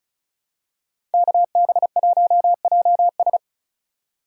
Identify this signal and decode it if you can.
Morse code, heard as a keyed tone, about 34 wpm, 700 Hz. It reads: KB1JS
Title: USA Call Signs - 5 Characters - 2x Character Spacing - 35wpm